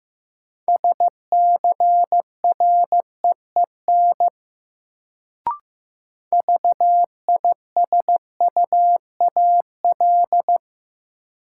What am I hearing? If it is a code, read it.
Morse code, heard as a keyed tone, about 16 wpm, 700 Hz. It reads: SCREEN VISUAL
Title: Sentences from Top 1000 Words - Review Part 6 - New Word Drill - 15wpm